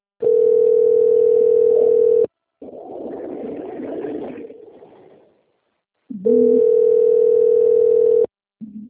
Ringback tone (North America), 440 + 480 Hz.
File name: rg-1600-unknown- - - .5968